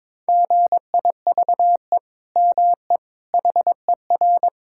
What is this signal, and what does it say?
Morse code, heard as a keyed tone, about 22 wpm, 700 Hz. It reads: GIVE ME HER
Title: Sentences from Top 1000 Words - Review Part 6 - 22wpm